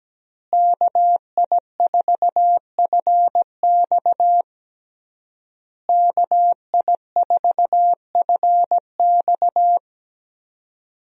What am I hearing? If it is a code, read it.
Morse code, heard as a keyed tone, about 17 wpm, 700 Hz. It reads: KI4FX KI4FX
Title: USA Call Signs - 5 Characters - Repeated 3x - 17wpm